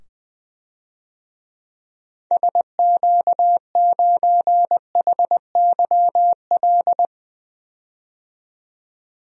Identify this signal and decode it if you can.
Morse code, heard as a keyed tone, about 20 wpm, 700 Hz. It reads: SQ9HYL
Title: DXCC-100 Call Signs - 6 Characters - 20wpm